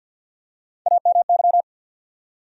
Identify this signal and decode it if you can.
Morse code, heard as a keyed tone, about 49 wpm, 700 Hz. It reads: AMY
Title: USA Common Names - 50wpm